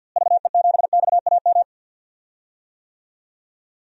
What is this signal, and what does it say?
Morse code, heard as a keyed tone, about 52 wpm, 700 Hz. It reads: VE7XAM
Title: DXCC-100 Call Signs - 6 Characters - Repeated 3x - 50wpm